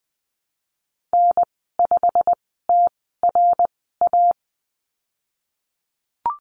Morse code, heard as a keyed tone, about 20 wpm, spaced as for 15 wpm, 700 Hz. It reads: N5TRA
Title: 088 Lesson 15g - Number 5 - Call Signs 20wpm